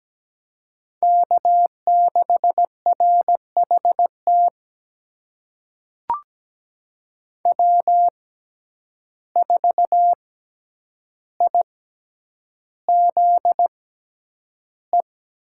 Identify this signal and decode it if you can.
Morse code, heard as a keyed tone, about 17 wpm, spaced as for 9 wpm, 700 Hz. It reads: K6RHT W4IZE
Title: USA Call Signs - 5 Characters - 6x Character Spacing - 17wpm